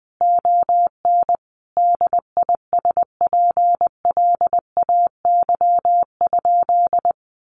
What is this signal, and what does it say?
Morse code, heard as a keyed tone, about 20 wpm, 700 Hz. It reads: ON DISPLAY?
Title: Sentences from Top 1200 Words 20wpm